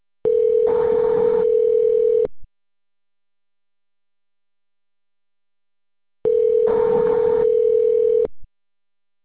Ringback tone (North America), 440 + 480 Hz.